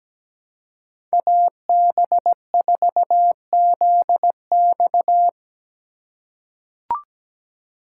Morse code, heard as a keyed tone, about 17 wpm, 700 Hz. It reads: AB4ZX
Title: USA Call Signs - 5 Characters - 2x Character Spacing - 17wpm